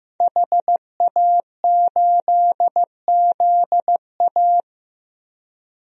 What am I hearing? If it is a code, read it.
Morse code, heard as a keyed tone, about 15 wpm, 700 Hz. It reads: HA8ZA
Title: DXCC-100 Call Signs - 5 Characters - Repeated 3x - 15wpm